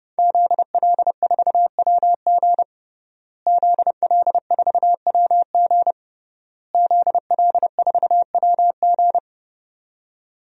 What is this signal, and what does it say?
Morse code, heard as a keyed tone, about 30 wpm, 700 Hz. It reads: ZL4WG ZL4WG ZL4WG
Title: DXCC-100 Call Signs - 5 Characters - Repeated 3x - 30wpm